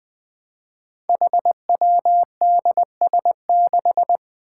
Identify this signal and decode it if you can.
Morse code, heard as a keyed tone, about 20 wpm, 700 Hz. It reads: HWDS6
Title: Five Challenging Letter-Number Combinations 20wpm